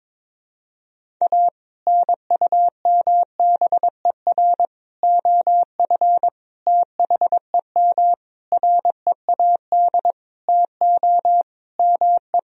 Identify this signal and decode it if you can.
Morse code, heard as a keyed tone, about 22 wpm, 700 Hz. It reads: A NUMBER OF THEM READ TO ME
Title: Sentences from Top 1000 Words - Review Part 1 - 22wpm